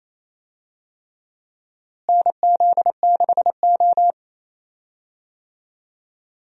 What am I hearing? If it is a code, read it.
Morse code, heard as a keyed tone, about 27 wpm, 700 Hz. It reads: NZ6O
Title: USA Call Signs - 4 Characters - Repeated 3x - 28wpm